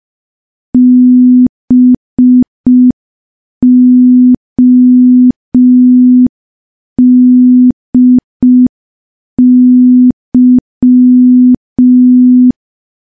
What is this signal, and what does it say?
Morse code, heard as a keyed tone, about 5 wpm, 250 Hz. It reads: BODY
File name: morse / morse-training-05WPM